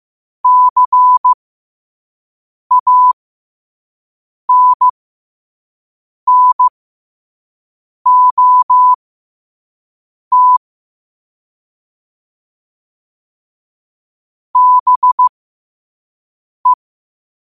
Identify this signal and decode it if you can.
Morse code, heard as a keyed tone, about 15 wpm, spaced as for 5 wpm, 1000 Hz. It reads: CANNOT BE
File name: qotdfiles / 5wpm